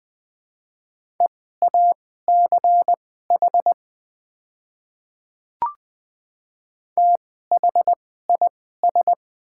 Morse code, heard as a keyed tone, about 20 wpm, spaced as for 14 wpm, 690 Hz. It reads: EACH THIS
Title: 2x Character Spacing - Top 100 Words - 20wpm